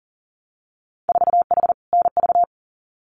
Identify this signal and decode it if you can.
Morse code, heard as a keyed tone, about 38 wpm, 700 Hz. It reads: 4H NV
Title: Winter Field Day Exchanges Encore 40wpm